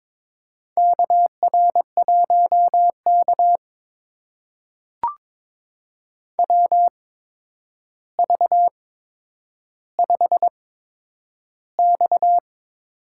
Morse code, heard as a keyed tone, about 22 wpm, spaced as for 9 wpm, 700 Hz. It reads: KR1K WV5X
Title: USA Call Signs - 4 Characters - 8x Character Spacing - 22wpm